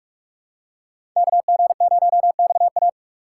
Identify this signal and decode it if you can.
Morse code, heard as a keyed tone, about 47 wpm, 700 Hz. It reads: KG0XA
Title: USA Call Signs - 5 Characters - 4x Character Spacing - 45wpm